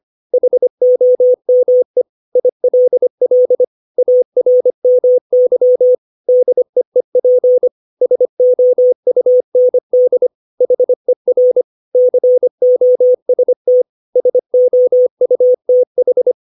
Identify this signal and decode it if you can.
Morse code, heard as a keyed tone, about 25 wpm, 500 Hz. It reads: HOME ILL ARMY DEEP SOUND HER COST SOUTH
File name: 632 words / 632 Words 25WPM